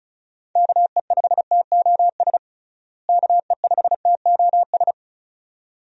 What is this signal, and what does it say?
Morse code, heard as a keyed tone, about 36 wpm, 700 Hz. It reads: KE5TOS KE5TOS
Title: USA Call Signs - 6 Characters - Repeated 2x - 35wpm